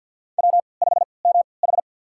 Morse code, heard as a keyed tone, about 46 wpm, spaced as for 27 wpm, 700 Hz. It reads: W5MH
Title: USA Call Signs - 4 Characters - 3x Character Spacing - 50wpm